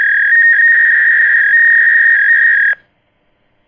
Fax machine handshake.